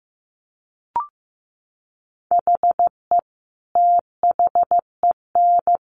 Morse code, heard as a keyed tone, about 15 wpm, 700 Hz. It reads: HE THEN
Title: Sentences from Top 1400 Words 15wpm